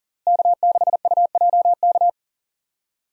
Morse code, heard as a keyed tone, about 40 wpm, 700 Hz. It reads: K6UJK
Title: USA Call Signs - 5 Characters - Repeated 2x - 40wpm